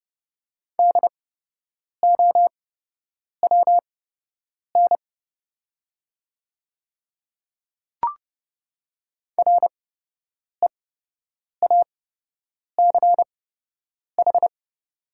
Morse code, heard as a keyed tone, about 30 wpm, spaced as for 8 wpm, 700 Hz. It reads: DOWN REACH
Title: 8x Character Spacing - Top 500 Words - 30wpm